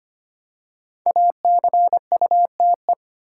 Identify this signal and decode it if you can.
Morse code, heard as a keyed tone, about 25 wpm, 700 Hz. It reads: ACUTE
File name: 5 Letter Words - First 3 Letters Given - 25wpm